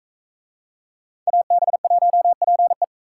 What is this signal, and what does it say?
Morse code, heard as a keyed tone, about 40 wpm, 700 Hz. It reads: AB1PE